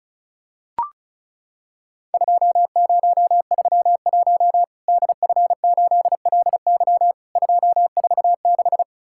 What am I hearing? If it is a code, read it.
Morse code, heard as a keyed tone, about 36 wpm, 700 Hz. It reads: 2031 DF8LY 246
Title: Worked All Europe DX Contest - QTC Exchange - 35wpm